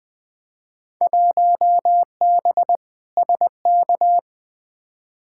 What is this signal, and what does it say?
Morse code, heard as a keyed tone, about 20 wpm, 700 Hz. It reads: 1B SK
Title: Field Day Exchanges Encore - 20wpm